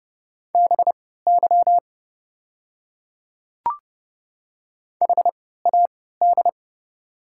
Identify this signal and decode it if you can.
Morse code, heard as a keyed tone, about 30 wpm, spaced as for 17 wpm, 700 Hz. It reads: BY HAD
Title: 3x Character Spacing - Top 100 Words - 30wpm